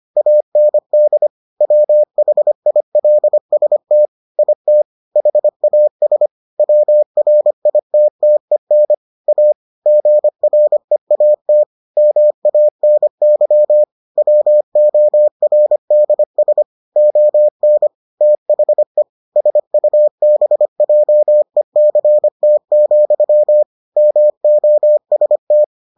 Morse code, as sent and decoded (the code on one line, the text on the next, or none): .- -. -.. / .-- .... .. .-.. ... - / .. - / .... .- ... / .-- .-. .. - - . -. / .- / --. .-. .- - / -- .- -. -.-- / .-- --- .-. -.. ... / --- -. / - .... / ... ..- -... .--- . -.-. - --..-- / -- --- ... -
AND WHILST IT HAS WRITTEN A GREAT MANY WORDS ON THE SUBJECT, MOST